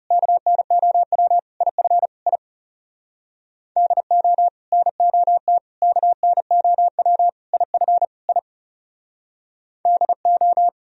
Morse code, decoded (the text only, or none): KNOW IF I DO NOT KNOW IF I DO